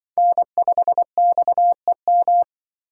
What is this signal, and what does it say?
Morse code, heard as a keyed tone, about 23 wpm, 700 Hz. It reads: N5XEM
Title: Speed Racing - USA 1x3 Call Signs 20wpm